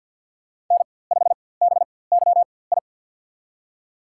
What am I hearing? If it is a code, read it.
Morse code, heard as a keyed tone, about 48 wpm, spaced as for 23 wpm, 700 Hz. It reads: N5BYI
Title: USA Call Signs - 5 Characters - 4x Character Spacing - 50wpm